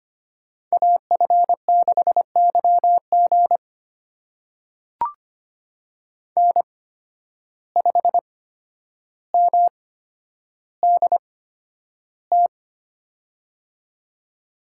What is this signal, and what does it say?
Morse code, heard as a keyed tone, about 25 wpm, spaced as for 11 wpm, 700 Hz. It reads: AF6YG N5MDT
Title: USA Call Signs - 5 Characters - 8x Character Spacing - 25wpm